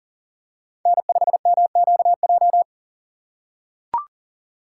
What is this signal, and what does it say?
Morse code, heard as a keyed tone, about 40 wpm, 700 Hz. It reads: N5MQJ
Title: USA Call Signs - 6x Character Spacing - 40wpm